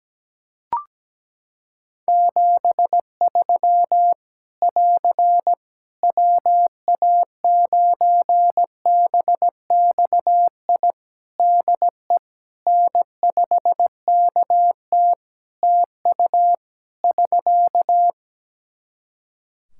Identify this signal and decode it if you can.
Morse code, heard as a keyed tone, about 17 wpm, 700 Hz. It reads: 73 <AR> WA9BXI DE N5KT TU <SK>
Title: Practice QSO - Exchange 4 - Ending - 17wpm